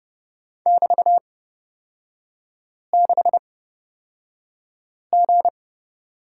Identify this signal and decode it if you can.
Morse code, heard as a keyed tone, about 30 wpm, spaced as for 5 wpm, 700 Hz. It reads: <BT>6G
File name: Lesson 36e - BT - Three Characters 30wpm